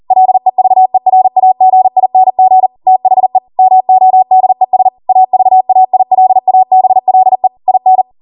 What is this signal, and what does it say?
Morse code, heard as a keyed tone, about 40 wpm, 750 Hz. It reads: LEVERAGING THE MODES AVAILABLE IN